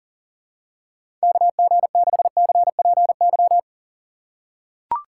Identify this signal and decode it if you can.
Morse code, heard as a keyed tone, about 40 wpm, 700 Hz. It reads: KG6CPY
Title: USA Call Signs - 6x Character Spacing - 40wpm